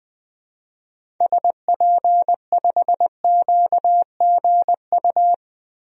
Morse code, heard as a keyed tone, about 20 wpm, 700 Hz. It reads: SP5QGU
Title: DXCC-100 Call Signs - 6 Characters - Repeated 3x - 20wpm